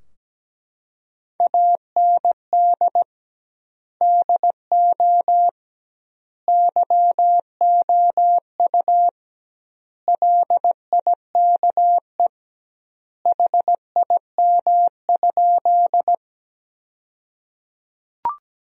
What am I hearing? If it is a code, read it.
Morse code, heard as a keyed tone, about 17 wpm, 700 Hz. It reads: AND DO YOU LIKE HIM?
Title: 2x Word-Spacing - Sentences from Top 100 Words - 17wpm